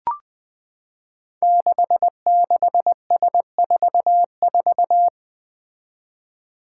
Morse code, heard as a keyed tone, about 20 wpm, 700 Hz. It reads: 66S44
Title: Five Challenging Letter-Number Combinations 20wpm